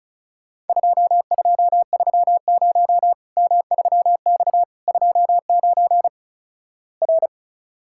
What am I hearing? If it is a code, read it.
Morse code, heard as a keyed tone, about 36 wpm, 700 Hz. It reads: 2230 M3X 29 R
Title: Worked All Europe DX Contest - QTC Exchange - 35wpm